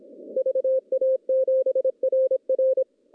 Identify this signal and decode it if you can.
Morse code, heard as a keyed tone, about 28 wpm, 500 Hz. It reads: VA7RR
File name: contests / CW-SS-k5tr- -0900